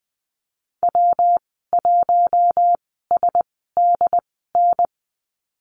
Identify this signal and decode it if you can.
Morse code, heard as a keyed tone, about 20 wpm, spaced as for 15 wpm, 700 Hz. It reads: W1SDN
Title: 128 Lesson 21g - Number 6 - Call Signs 20wpm